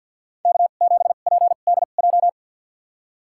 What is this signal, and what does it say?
Morse code, heard as a keyed tone, about 47 wpm, spaced as for 35 wpm, 700 Hz. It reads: K7PDJ
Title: USA Call Signs - 5 Characters - 2x Character Spacing - 50wpm